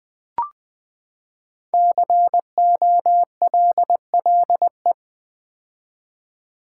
Morse code, heard as a keyed tone, about 20 wpm, 700 Hz. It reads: COLLE
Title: Build A Word - 5 Letters At A Time - 10-Letter Words - 20wpm